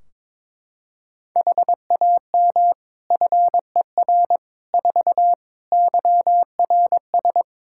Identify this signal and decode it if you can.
Morse code, heard as a keyed tone, about 23 wpm, 700 Hz. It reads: HAM FER 4 YRS
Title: Practice QSO - Exchange 3 - More Info - 22wpm